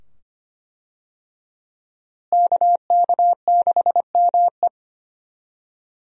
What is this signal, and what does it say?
Morse code, heard as a keyed tone, about 25 wpm, 700 Hz. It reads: KK6ME
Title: USA Call Signs - 5 Characters - Repeated 2x - 25wpm